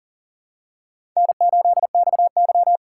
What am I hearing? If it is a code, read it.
Morse code, heard as a keyed tone, about 40 wpm, 700 Hz. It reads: N8XY
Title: USA Call Signs - 4 Characters - 4x Character Spacing - 40wpm